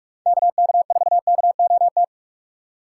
Morse code, heard as a keyed tone, about 46 wpm, 700 Hz. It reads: KK4KOT
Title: USA Call Signs - 6 Characters - 4x Character Spacing - 45wpm